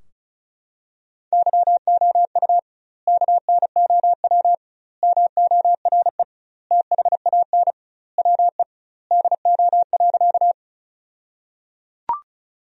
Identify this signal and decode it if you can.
Morse code, heard as a keyed tone, about 36 wpm, 700 Hz. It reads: YOU KNOW MORE THAN WE DO.
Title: 2x Word-Spacing - Sentences from Top 100 Words - 35wpm